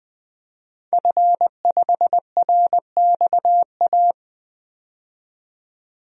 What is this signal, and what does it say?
Morse code, heard as a keyed tone, about 21 wpm, 700 Hz. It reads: F5RXA
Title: DXCC-100 Call Signs - 5 Characters - 20wpm